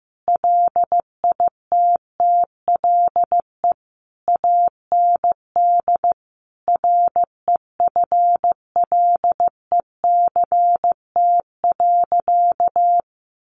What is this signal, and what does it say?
Morse code, heard as a keyed tone, about 15 wpm, 700 Hz. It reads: LITTLE AND REFLECT.
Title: Sentences from Top 1500 Words 15wpm